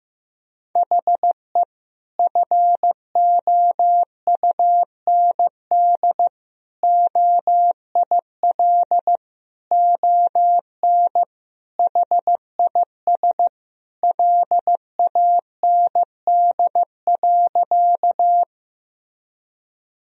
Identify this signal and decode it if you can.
Morse code, heard as a keyed tone, about 15 wpm, 700 Hz. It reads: HE FOUND OIL ON HIS LAND.
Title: Sentences from Top 1000 Words - Review Part 4 - 15wpm